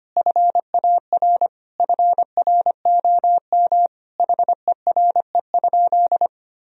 Morse code, heard as a keyed tone, about 25 wpm, 700 Hz. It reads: FAR FROM HERE?
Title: Sentences from Top 1000 Words - Review Part 4 - 25wpm